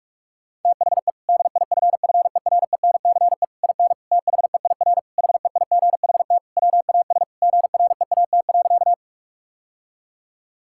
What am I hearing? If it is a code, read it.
Morse code, heard as a keyed tone, about 46 wpm, 700 Hz. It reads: THE DIFFERENCE IN THEIR HEIGHT WAS GREAT.
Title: Sentences from Top 1600 Words 45wpm